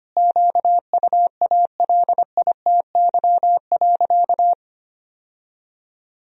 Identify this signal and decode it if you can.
Morse code, heard as a keyed tone, about 25 wpm, 700 Hz. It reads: QUALITY.
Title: Sentences from Top 1000 Words - Review Part 4 - 25wpm